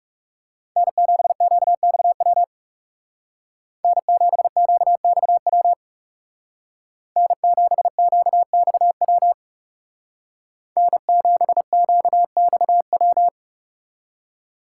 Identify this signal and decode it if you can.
Morse code, heard as a keyed tone, about 35 wpm, 700 Hz. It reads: N7QXW N7QXW N7QXW N7QXW
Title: Speed Racing - USA 1x3 Call Signs 30wpm